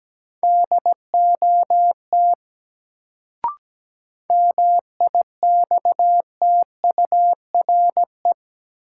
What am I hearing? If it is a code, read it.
Morse code, heard as a keyed tone, about 17 wpm, 700 Hz. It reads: DOT MIXTURE
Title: Sentences from Top 1500 Words - New Word Drill - 17wpm